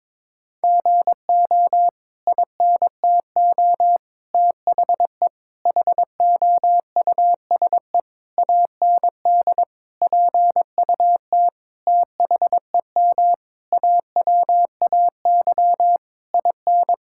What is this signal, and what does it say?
Morse code, heard as a keyed tone, about 22 wpm, 700 Hz. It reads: GO INTO THE HOUSE AND PUT THEM AWAY IN
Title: Sentences from Top 1000 Words - Review Part 5 - 22wpm